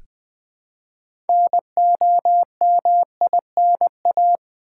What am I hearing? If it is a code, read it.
Morse code, heard as a keyed tone, about 20 wpm, 700 Hz. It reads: NOMINA
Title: Build A Word - 3 Letters At A Time - 8-Letter Words - 20wpm